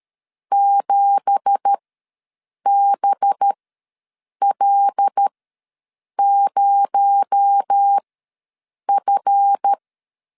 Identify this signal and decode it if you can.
Morse code, heard as a keyed tone, about 13 wpm, spaced as for 7 wpm, 800 Hz. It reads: 7BL0F
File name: LEZIONI / 9 2867 ws 12wpm 3431